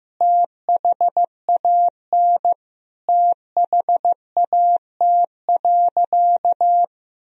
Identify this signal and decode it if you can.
Morse code, heard as a keyed tone, about 16 wpm, 700 Hz. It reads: THAN THAT.
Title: Sentences from Top 1000 Words - Review Part 2 - 15wpm